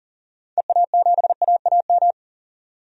Morse code, heard as a keyed tone, about 40 wpm, 700 Hz. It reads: EA7AAM
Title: DXCC-100 Call Signs - 6 Characters - 40wpm